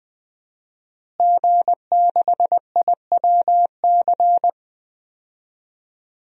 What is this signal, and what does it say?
Morse code, heard as a keyed tone, about 20 wpm, 700 Hz. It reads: G6IWC
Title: DXCC-100 Call Signs - 5 Characters - Repeated 3x - 20wpm